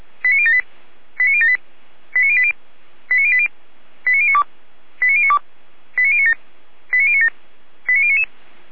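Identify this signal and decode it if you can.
Five-tone selective call (ZVEI), heard as quick runs of five tones, tones of about 70 ms. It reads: 89078 89078 89080 89080 89092 89092 89097 89097 89099
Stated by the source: ZVEI five-tone selective call received via SDR.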